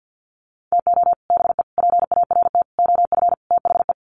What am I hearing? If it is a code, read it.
Morse code, heard as a keyed tone, about 50 wpm, 700 Hz. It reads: TO BE PART OF THE
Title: Sets of 5 Words 50wpm